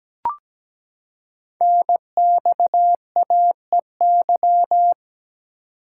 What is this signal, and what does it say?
Morse code, heard as a keyed tone, about 17 wpm, 700 Hz. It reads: NXAEY
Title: Code Groups - 5 Letters - 17wpm